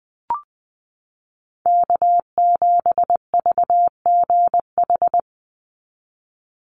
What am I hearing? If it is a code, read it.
Morse code, heard as a keyed tone, about 20 wpm, 700 Hz. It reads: K7VGH
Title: USA Call Signs 20wpm